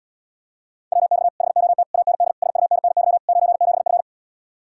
Morse code, heard as a keyed tone, about 78 wpm, 700 Hz. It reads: WX HOT AND SUNNY 97F